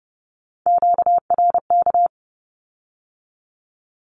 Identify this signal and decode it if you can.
Morse code, heard as a keyed tone, about 30 wpm, 700 Hz. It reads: QRK
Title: Q-Codes with Spoken Meaning - 30wpm